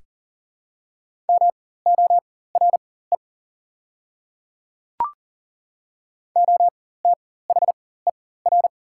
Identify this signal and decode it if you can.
Morse code, heard as a keyed tone, about 40 wpm, spaced as for 19 wpm, 700 Hz. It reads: MORE OTHER
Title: 4x Character Spacing - Top 100 Words - 40wpm